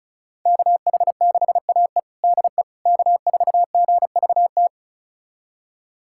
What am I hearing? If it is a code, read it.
Morse code, heard as a keyed tone, about 36 wpm, 700 Hz. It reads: KH6AE DE K4GVT
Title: USA Callsign DE Callsign - 5 Characters - 35wpm